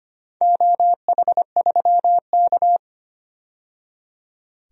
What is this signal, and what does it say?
Morse code, heard as a keyed tone, about 25 wpm, 700 Hz. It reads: OH3K
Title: DXCC-100 Call Signs - 4 Characters - 25wpm